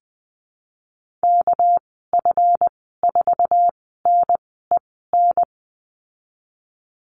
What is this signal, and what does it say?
Morse code, heard as a keyed tone, about 20 wpm, spaced as for 15 wpm, 700 Hz. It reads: KF4NEN
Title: 200 Lesson 32g - Letter k - Call Signs 20wpm